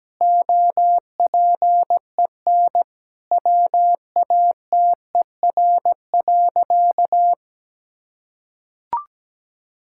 Morse code, heard as a keyed tone, about 17 wpm, 700 Hz. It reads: OPEN WATER.
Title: Sentences from Top 1000 Words - Review Part 6 - 17wpm